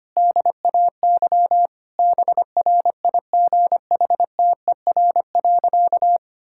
Morse code, heard as a keyed tone, about 25 wpm, 700 Hz. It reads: DAY BRIGHTER.
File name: practice / 25wpm / Difficult Sentences with Hints - 25wpm